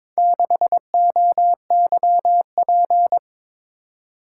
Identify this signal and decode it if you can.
Morse code, heard as a keyed tone, about 22 wpm, 700 Hz. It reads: 6OYP
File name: practice / 22wpm / Four Letter-Number Combinations - 22wpm